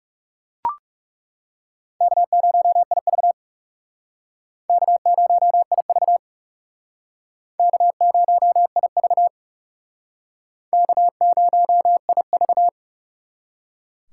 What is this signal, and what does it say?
Morse code, heard as a keyed tone, about 39 wpm, 700 Hz. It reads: K0IV K0IV K0IV K0IV